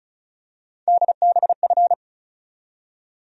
Morse code, heard as a keyed tone, about 36 wpm, 700 Hz. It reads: DBF
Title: USA Call Sign Postfixes - 3 Letters 35wpm